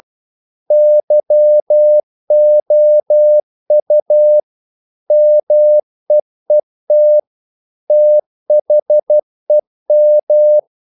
Morse code, heard as a keyed tone, about 12 wpm, 600 Hz. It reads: YOU MEET THEM
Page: English quotations in CW (telegraphy - morse)